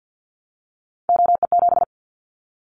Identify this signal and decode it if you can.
Morse code, heard as a keyed tone, about 50 wpm, 700 Hz. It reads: OE7
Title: Callsign Prefixes for Top 100 DXCC - 50wpm